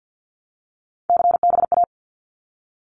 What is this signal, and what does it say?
Morse code, heard as a keyed tone, about 50 wpm, 700 Hz. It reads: CBA
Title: Abbreviations Repeated 3x - 50wpm